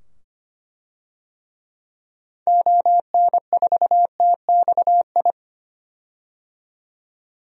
Morse code, heard as a keyed tone, about 25 wpm, 700 Hz. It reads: ON4TXI
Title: DXCC-100 Call Signs - 6 Characters - 25wpm